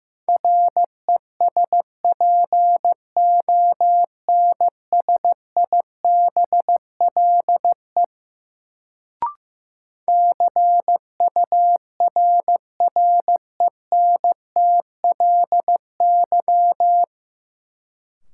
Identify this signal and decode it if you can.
Morse code, heard as a keyed tone, about 15 wpm, 700 Hz. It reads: RESPONSIBLE CURRENTLY
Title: Sentences from Top 1000 Words - Review Part 7 - New Word Drill - 15wpm